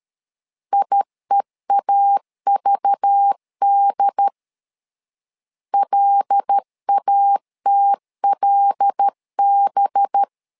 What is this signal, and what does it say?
Morse code, heard as a keyed tone, about 13 wpm, 800 Hz. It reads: IEAVD LATLB
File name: LEZIONI / 5 LBVC / 5 LBVC 12wpm 1955